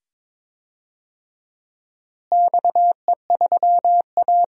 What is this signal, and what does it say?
Morse code, heard as a keyed tone, about 22 wpm, 700 Hz. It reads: XE3A
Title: DXCC-100 Call Signs - 4 Characters - Repeated 2x - 22wpm